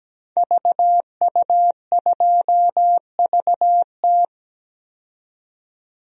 Morse code, heard as a keyed tone, about 17 wpm, 700 Hz. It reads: VU2VT
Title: DXCC-100 Call Signs - 5 Characters - 17wpm